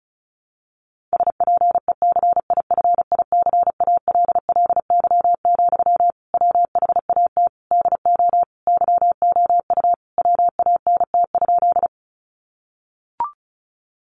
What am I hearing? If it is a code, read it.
Morse code, heard as a keyed tone, about 34 wpm, 700 Hz. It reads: SPECIFICALLY, WHAT DO YOU WANT?
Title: Sentences from Top 1100 Words 35wpm